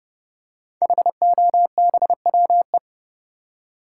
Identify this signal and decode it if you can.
Morse code, heard as a keyed tone, about 31 wpm, 700 Hz. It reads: HOBWE
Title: Code Groups - 5 Letters - 30wpm